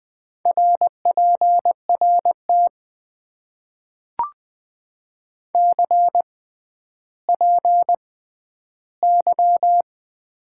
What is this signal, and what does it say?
Morse code, heard as a keyed tone, about 20 wpm, spaced as for 11 wpm, 700 Hz. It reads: RPRT CPY
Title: Top 90 QSO Elements - 6x Character Spacing - 20wpm